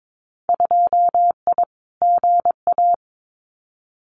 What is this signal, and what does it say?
Morse code, heard as a keyed tone, about 22 wpm, 700 Hz. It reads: 2I GA
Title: Winter Field Day Exchanges 22wpm